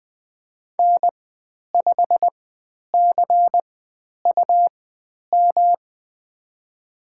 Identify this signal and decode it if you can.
Morse code, heard as a keyed tone, about 20 wpm, spaced as for 10 wpm, 700 Hz. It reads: N5CUM
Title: 109 Lesson 18g - Letter m - Call Signs 20wpm